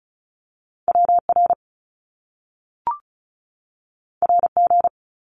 Morse code, heard as a keyed tone, about 36 wpm, 700 Hz. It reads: WR RG
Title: Top Two Letter Combinations in Words 35wpm